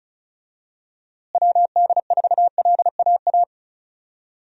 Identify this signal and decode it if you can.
Morse code, heard as a keyed tone, about 34 wpm, 700 Hz. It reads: WD4LAA
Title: USA Call Signs - 6 Characters - Repeated 3x - 35wpm